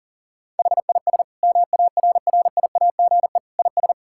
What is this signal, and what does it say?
Morse code, heard as a keyed tone, about 40 wpm, 700 Hz. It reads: HIS MARRIAGE IS
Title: Sentences from Top 1000 Words - Review Part 6 - 40wpm